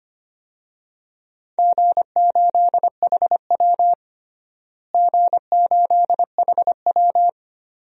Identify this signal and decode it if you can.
Morse code, heard as a keyed tone, about 25 wpm, 700 Hz. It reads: G8HW G8HW
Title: DXCC-100 Call Signs - 4 Characters - Repeated 2x - 25wpm